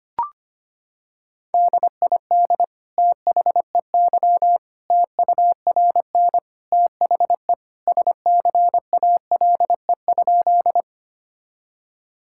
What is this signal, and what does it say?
Morse code, heard as a keyed tone, about 25 wpm, 700 Hz. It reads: DID THEY TURN THE SCALE?
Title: Sentences from Top 1000 Words - Review Part 5 - 25wpm